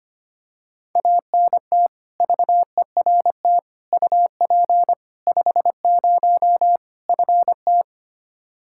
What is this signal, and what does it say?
Morse code, heard as a keyed tone, about 25 wpm, 700 Hz. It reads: ANT VERT UP 50 FT
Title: Practice QSO - Exchange 2 - Info - 25wpm